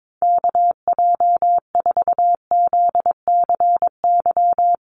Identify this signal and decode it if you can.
Morse code, heard as a keyed tone, about 22 wpm, 700 Hz. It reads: KJ4ZCY
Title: USA Call Signs Repeated 2x 22wpm